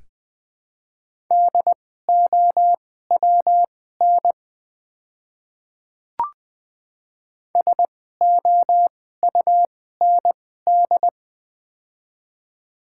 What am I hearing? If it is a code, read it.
Morse code, heard as a keyed tone, about 20 wpm, spaced as for 14 wpm, 700 Hz. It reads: DOWN SOUND
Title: 2x Character Spacing - Top 100 Words - 20wpm